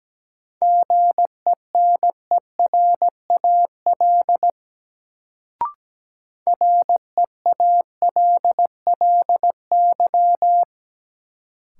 Morse code, heard as a keyed tone, about 17 wpm, 700 Hz. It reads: GENERAL REALLY
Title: Sentences from Top 600 Words - New Word Drill - 17wpm